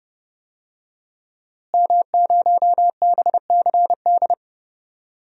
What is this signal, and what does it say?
Morse code, heard as a keyed tone, about 30 wpm, 700 Hz. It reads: M0BCD
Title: DXCC-100 Call Signs - 5 Characters - Repeated 3x - 30wpm